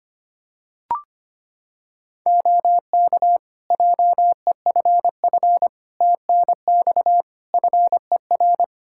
Morse code, heard as a keyed tone, about 25 wpm, 700 Hz. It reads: OK JEFF TNX FER
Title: Practice QSO - Exchange 4 - Ending - 25wpm